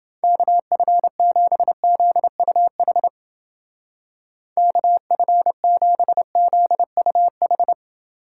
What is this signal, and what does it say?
Morse code, heard as a keyed tone, about 29 wpm, 700 Hz. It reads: KF7ZUH KF7ZUH